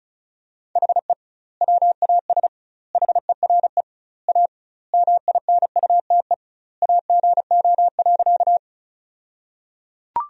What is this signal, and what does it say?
Morse code, heard as a keyed tone, about 36 wpm, 700 Hz. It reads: HE WAS HERE A MINUTE AGO.
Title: 2x Word-Spacing - Sentences from Top 500 Words - 35wpm